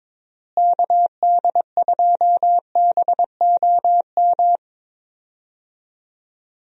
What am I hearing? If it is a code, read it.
Morse code, heard as a keyed tone, about 22 wpm, 700 Hz. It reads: KD2BOM
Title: USA Call Signs - 6 Characters - Repeated 3x - 22wpm